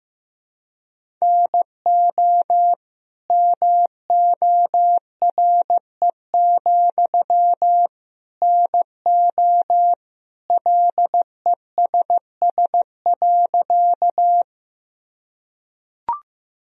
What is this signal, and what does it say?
Morse code, heard as a keyed tone, about 15 wpm, 700 Hz. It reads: NO MORE, NO LESS.
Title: Sentences from Top 1000 Words - Review Part 2 - 15wpm